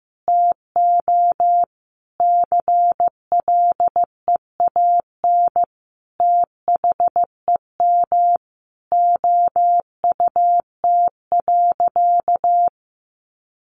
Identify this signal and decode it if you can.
Morse code, heard as a keyed tone, about 15 wpm, 700 Hz. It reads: TO CLEAN THEM OUT.
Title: Sentences from Top 1500 Words 15wpm